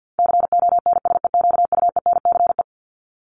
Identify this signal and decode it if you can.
Morse code, heard as a keyed tone, about 50 wpm, 700 Hz. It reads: CONSEQUENCE
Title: LY Suffix - Top 100 Words with Base Words 50wpm